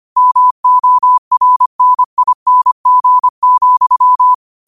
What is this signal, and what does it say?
Morse code, heard as a keyed tone, about 25 wpm, 1000 Hz. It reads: MORNING,
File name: 25wpm